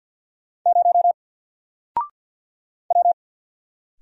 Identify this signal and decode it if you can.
Morse code, heard as a keyed tone, about 48 wpm, 700 Hz. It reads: 0 W
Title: Single Character (Letter:Number:Punctuation:Prosign) 50wpm